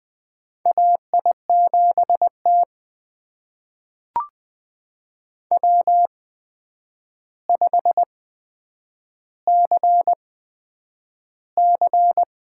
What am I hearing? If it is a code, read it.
Morse code, heard as a keyed tone, about 20 wpm, spaced as for 9 wpm, 700 Hz. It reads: AI7T W5CC
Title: USA Call Signs - 4 Characters - 8x Character Spacing - 20wpm